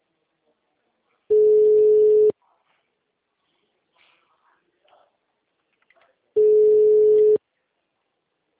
Ringback tone (Europe, 425 Hz).
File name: out- -2011- - - .20